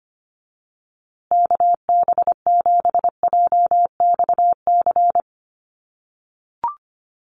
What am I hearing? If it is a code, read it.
Morse code, heard as a keyed tone, about 25 wpm, 700 Hz. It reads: KB7JXC
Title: USA Call Signs Repeated 3x 25wpm